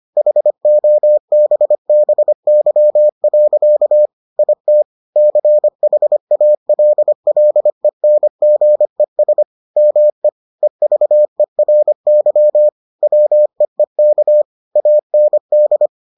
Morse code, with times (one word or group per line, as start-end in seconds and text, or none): 0.2-4.1 HOBBY.
4.4-4.8 IT
5.2-9.4 CHALLENGES
9.8-10.3 ME
10.6-12.7 EVERY
13.0-14.4 WEEK
14.8-15.9 AND